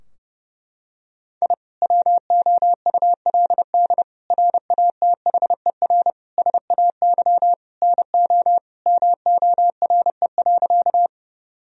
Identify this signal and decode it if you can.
Morse code, heard as a keyed tone, about 32 wpm, 700 Hz. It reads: I WOULD RATHER SAY NO MORE.
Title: Sentences from Top 1000 Words - Review Part 4 - 30wpm